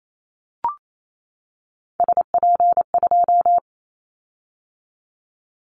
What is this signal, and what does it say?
Morse code, heard as a keyed tone, about 28 wpm, 700 Hz. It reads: SP2
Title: Callsign Prefixes for Top 25 DXCC - 28wpm